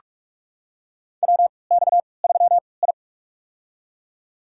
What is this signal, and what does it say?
Morse code, heard as a keyed tone, about 45 wpm, spaced as for 25 wpm, 700 Hz. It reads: WX3I
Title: USA Call Signs - 3x Character Spacing - 45wpm